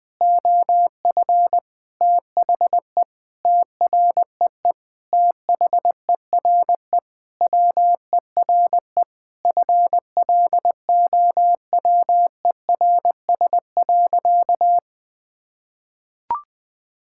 Morse code, heard as a keyed tone, about 20 wpm, 700 Hz. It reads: OF THE TREE THERE WERE FLOWERS.
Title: Sentences with 9-Word Phrases - 20wpm